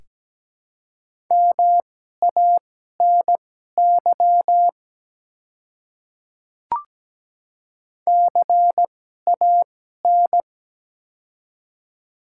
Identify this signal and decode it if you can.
Morse code, heard as a keyed tone, about 17 wpm, spaced as for 12 wpm, 700 Hz. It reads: MANY CAN
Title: 2x Character Spacing - Top 200 Words - 17wpm